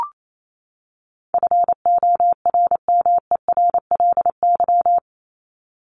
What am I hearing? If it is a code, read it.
Morse code, heard as a keyed tone, about 28 wpm, 700 Hz. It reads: FORMERLY
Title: LY Suffix - Top 100 Words with Base Words 28wpm